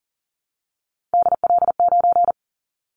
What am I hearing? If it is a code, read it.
Morse code, heard as a keyed tone, about 42 wpm, 700 Hz. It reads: DL9
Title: Callsign Prefixes for Top 50 DXCC - 40wpm